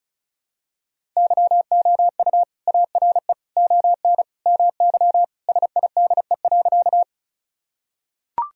Morse code, heard as a keyed tone, about 36 wpm, 700 Hz. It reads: YOU ARE ON MY SIDE.
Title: Sentences from Top 1000 Words - Review Part 1 - 35wpm